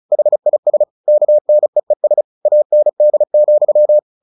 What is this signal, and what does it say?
Morse code, heard as a keyed tone, about 34 wpm, 600 Hz. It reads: HIS KNEES AND,
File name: the old man and the sea / the old man and the sea 1 35wpm